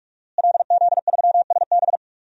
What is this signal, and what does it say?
Morse code, heard as a keyed tone, about 46 wpm, 700 Hz. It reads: LZ3SB
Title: DXCC-100 Call Signs - 5 Characters - Repeated 3x - 45wpm